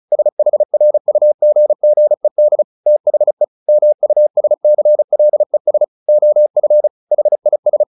Morse code, heard as a keyed tone, about 34 wpm, 600 Hz. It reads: SHRUGGED THE MUSCLES OF HIS
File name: the old man and the sea / the old man and the sea 1 35wpm